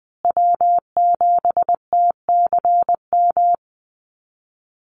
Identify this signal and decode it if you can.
Morse code, heard as a keyed tone, about 20 wpm, 700 Hz. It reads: W7TCM
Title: USA Call Signs Repeated 2x 20wpm